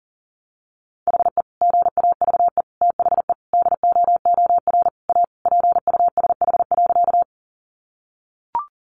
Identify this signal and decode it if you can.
Morse code, heard as a keyed tone, about 40 wpm, 700 Hz. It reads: HE GAVE THE DOOR A PUSH.
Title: Sentences from Top 1300 Words 40wpm